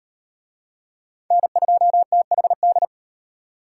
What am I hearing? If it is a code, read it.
Morse code, heard as a keyed tone, about 32 wpm, 700 Hz. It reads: N2THD